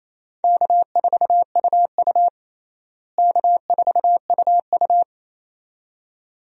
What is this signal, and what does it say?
Morse code, heard as a keyed tone, about 28 wpm, 700 Hz. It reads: K4UU K4UU
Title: USA Call Signs - 4 Characters - Repeated 2x - 28wpm